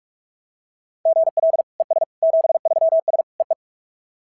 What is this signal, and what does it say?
Morse code, heard as a keyed tone, about 46 wpm, 690 Hz. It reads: GL ES 73S EE
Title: SOTA QSO Practice 45wpm